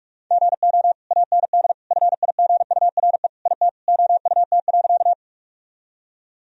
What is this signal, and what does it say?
Morse code, heard as a keyed tone, about 42 wpm, 700 Hz. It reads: GO AND FIGURE IT OUT.
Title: Sentences from Top 1000 Words - Review Part 2 - 45wpm